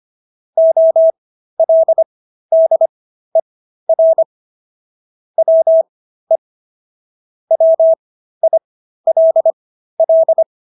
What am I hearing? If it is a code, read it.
Morse code, heard as a keyed tone, about 25 wpm, spaced as for 13 wpm, 650 Hz. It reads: OLDER WE WILL